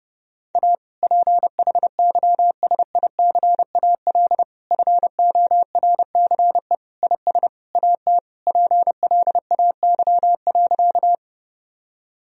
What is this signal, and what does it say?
Morse code, heard as a keyed tone, about 30 wpm, 700 Hz. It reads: A PHYSICAL FORCE IS AT PLAY.
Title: Sentences from Top 1000 Words - Review Part 4 - 30wpm